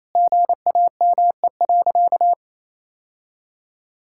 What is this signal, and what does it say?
Morse code, heard as a keyed tone, about 28 wpm, 700 Hz. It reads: GAME.
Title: Sentences from Top 1000 Words - Review Part 5 - 28wpm